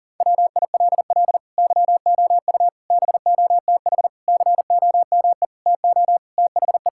Morse code, heard as a keyed tone, about 40 wpm, 700 Hz. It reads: WILL YOU BOTH COME TO THE
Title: Sentences from Top 1600 Words 40wpm